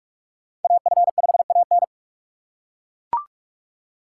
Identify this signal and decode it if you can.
Morse code, heard as a keyed tone, about 41 wpm, 700 Hz. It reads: AF5AN
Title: USA Call Signs - 5 Characters - 3x Character Spacing - 45wpm